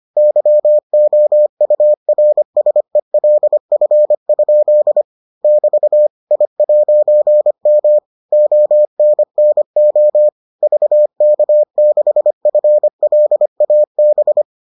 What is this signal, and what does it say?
Morse code, heard as a keyed tone, about 25 wpm, 600 Hz. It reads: YOURSELF? <BT> I'M ONNO VK6FLAB